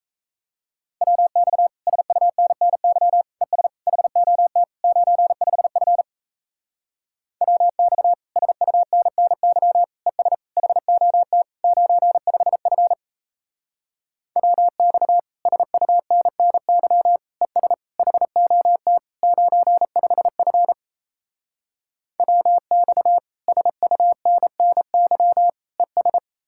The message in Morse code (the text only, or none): WX SUNNY ES HOT 95F WX SUNNY ES HOT 95F WX SUNNY ES HOT 95F WX SUNNY ES